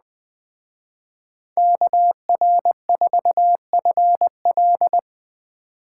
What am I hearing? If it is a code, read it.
Morse code, heard as a keyed tone, about 20 wpm, 700 Hz. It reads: KR4FL
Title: Speed Racing - USA 2x2 Call Signs 17wpm